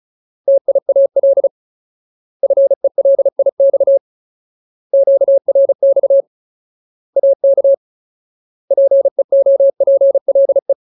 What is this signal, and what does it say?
Morse code, heard as a keyed tone, about 34 wpm, 550 Hz. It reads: TIAL FELIX QRX AK PEOPLE
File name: Ultimate Word List / 35WPM Ultimate